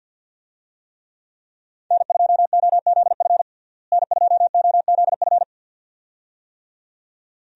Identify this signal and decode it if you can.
Morse code, heard as a keyed tone, about 48 wpm, 700 Hz. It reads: N2OZF N2OZF
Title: USA Call Signs - 5 Characters - Repeated 2x - 50wpm